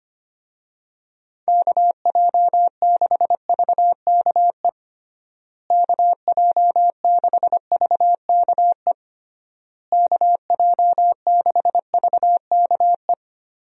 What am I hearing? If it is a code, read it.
Morse code, heard as a keyed tone, about 25 wpm, 700 Hz. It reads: KJ6VKE KJ6VKE KJ6VKE
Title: USA Call Signs - 6 Characters - Repeated 3x - 25wpm